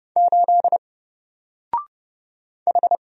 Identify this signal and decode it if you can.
Morse code, heard as a keyed tone, about 30 wpm, 700 Hz. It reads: 8 H
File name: Single-Letter-Number-30wpm